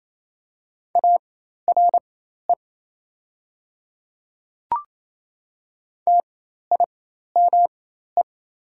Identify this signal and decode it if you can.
Morse code, heard as a keyed tone, about 28 wpm, spaced as for 13 wpm, 700 Hz. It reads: ARE TIME
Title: 4x Character Spacing - Top 100 Words - 28wpm